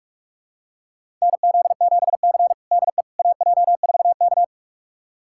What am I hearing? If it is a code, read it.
Morse code, heard as a keyed tone, about 41 wpm, 700 Hz. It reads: NZ7C DE AJ4K
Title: USA Callsign DE Callsign - 4 Characters - 45wpm